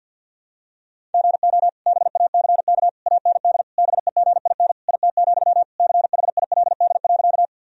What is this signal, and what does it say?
Morse code, heard as a keyed tone, about 51 wpm, 700 Hz. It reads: GO BACK AND BEGIN IT, CHILD.
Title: Sentences from Top 1000 Words - Review Part 1 - 50wpm